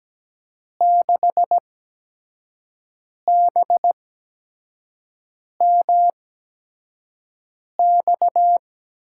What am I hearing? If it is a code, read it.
Morse code, heard as a keyed tone, about 17 wpm, spaced as for 5 wpm, 700 Hz. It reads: 6BMX
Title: USA Call Signs - 5 Characters - 8x Character Spacing - 17wpm